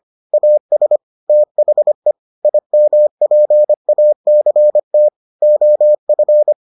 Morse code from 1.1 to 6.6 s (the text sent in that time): THE IMPACT OF